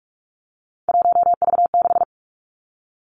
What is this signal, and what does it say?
Morse code, heard as a keyed tone, about 46 wpm, 700 Hz. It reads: 1V6
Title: Three Challenging Letter-Number Combinations 45wpm